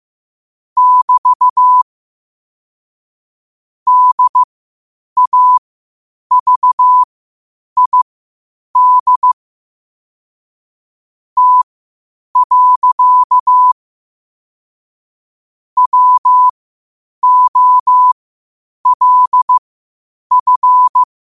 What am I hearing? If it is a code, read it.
Morse code, heard as a keyed tone, about 15 wpm, spaced as for 8 wpm, 1000 Hz. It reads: <BT> DAVID T. WOLF
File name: 8wpm